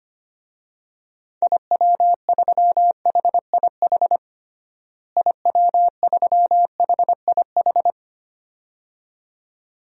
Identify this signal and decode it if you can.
Morse code, heard as a keyed tone, about 25 wpm, 700 Hz. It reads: IW3HIH IW3HIH
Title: DXCC-100 Call Signs - 6 Characters - Repeated 2x - 25wpm